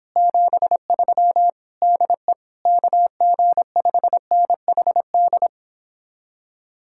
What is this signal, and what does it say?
Morse code, heard as a keyed tone, about 29 wpm, 700 Hz. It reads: 73 DE KG5NHD